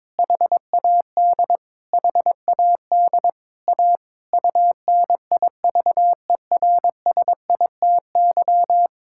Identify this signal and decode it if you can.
Morse code, heard as a keyed tone, about 22 wpm, 700 Hz. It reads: HAD HAD A UNIVERSITY
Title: Sentences from Top 1000 Words - Review Part 6 - 22wpm